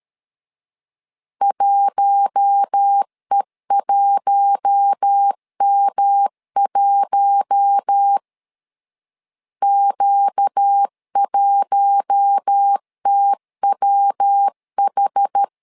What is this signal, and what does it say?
Morse code, heard as a keyed tone, about 13 wpm, 800 Hz. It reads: 1E1M1 Q1TWH
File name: LEZIONI / 4 Q1 12wpm 2028